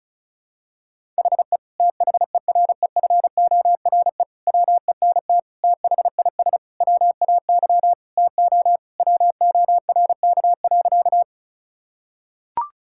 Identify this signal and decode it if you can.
Morse code, heard as a keyed tone, about 34 wpm, 690 Hz. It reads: HE THEREFORE WENT THIS WAY TO WORK.
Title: Sentences from Top 1000 Words - Review Part 5 - 35wpm